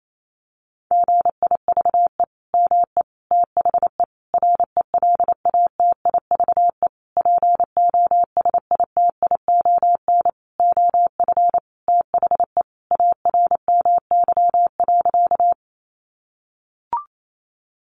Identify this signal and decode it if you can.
Morse code, heard as a keyed tone, about 28 wpm, 700 Hz. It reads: GIVE ME THE RELATIVE POSITION OF THE ARMY.
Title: Sentences from Top 1400 Words 28wpm